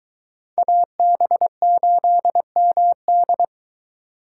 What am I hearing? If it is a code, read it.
Morse code, heard as a keyed tone, about 20 wpm, 700 Hz. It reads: AB8MD